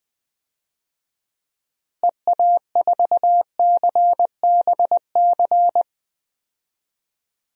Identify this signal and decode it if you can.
Morse code, heard as a keyed tone, about 20 wpm, 700 Hz. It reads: EA4CBC
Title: DXCC-100 Call Signs - 6 Characters - Repeated 2x - 20wpm